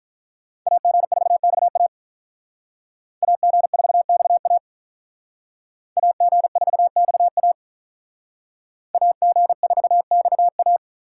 Morse code, heard as a keyed tone, about 42 wpm, 700 Hz. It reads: AG4XA AG4XA AG4XA AG4XA
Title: Speed Racing - USA Call Signs 35wpm